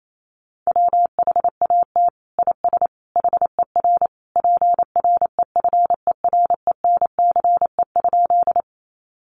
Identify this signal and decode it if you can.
Morse code, heard as a keyed tone, about 28 wpm, 700 Hz. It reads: WHAT IS HER PREFERENCE?
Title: Sentences from Top 1500 Words 28wpm